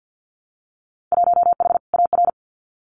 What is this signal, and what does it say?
Morse code, heard as a keyed tone, about 50 wpm, 700 Hz. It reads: 1H AR
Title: Winter Field Day Exchanges 50wpm